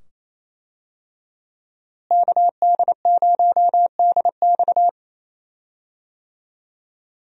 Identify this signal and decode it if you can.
Morse code, heard as a keyed tone, about 28 wpm, 700 Hz. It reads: KD0DX
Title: USA Call Signs - 5 Characters - Repeated 2x - 28wpm